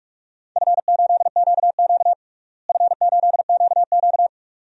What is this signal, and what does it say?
Morse code, heard as a keyed tone, about 43 wpm, 700 Hz. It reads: F8QQ F8QQ
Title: DXCC-100 Call Signs - 4 Characters - Repeated 2x - 45wpm